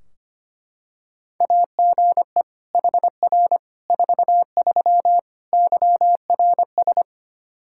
Morse code, heard as a keyed tone, about 25 wpm, 700 Hz. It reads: AGE HR 43 YRS
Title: K1USN SST QSO Practice 25wpm